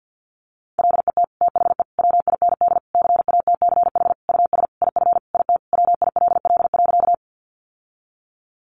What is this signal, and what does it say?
Morse code, heard as a keyed tone, about 50 wpm, 700 Hz. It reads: LET THE WIND CATCH US IF IT WILL.
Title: Sentences from Top 1500 Words 50wpm